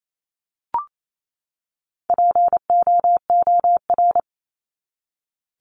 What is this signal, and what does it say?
Morse code, heard as a keyed tone, about 28 wpm, 700 Hz. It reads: POOR
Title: Top 751-1000 Words 28wpm